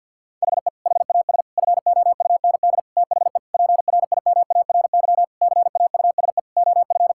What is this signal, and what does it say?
Morse code, heard as a keyed tone, about 50 wpm, 700 Hz. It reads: HE HAS FOUND THE PRIMARY CAUSE OF
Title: Sentences from Top 1000 Words - Review Part 5 - 50wpm